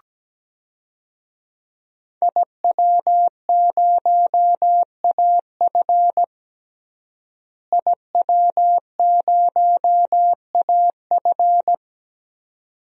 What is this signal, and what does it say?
Morse code, heard as a keyed tone, about 17 wpm, 700 Hz. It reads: IW0AF IW0AF
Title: DXCC-100 Call Signs - 5 Characters - Repeated 2x - 17wpm